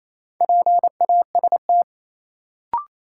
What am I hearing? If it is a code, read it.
Morse code, heard as a keyed tone, about 28 wpm, 700 Hz. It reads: PAST
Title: Sentences from Top 600 Words - New Word Drill - 28wpm